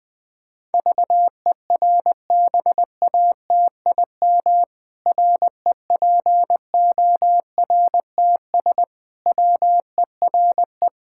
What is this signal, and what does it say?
Morse code, heard as a keyed tone, about 20 wpm, 700 Hz. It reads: VERBATIM REPORTS WERE
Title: Pangram Challenge v2 - 20wpm